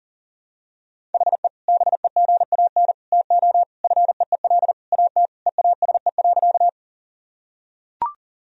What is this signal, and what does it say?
Morse code, heard as a keyed tone, about 40 wpm, 700 Hz. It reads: HE BEGAN TO FEEL AT EASE.
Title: Sentences from Top 1000 Words - Review Part 1 - 40wpm